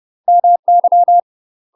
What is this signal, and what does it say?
Morse code, heard as a keyed tone, about 30 wpm, 700 Hz. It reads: MY
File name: UK MASTER QSOs TRAINING/30WPM